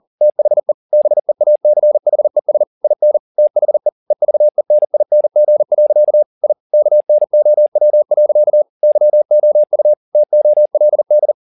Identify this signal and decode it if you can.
Morse code, heard as a keyed tone, about 40 wpm, 600 Hz. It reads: THE BEACHES IN THE EVENING. I KNOW. YOU TOLD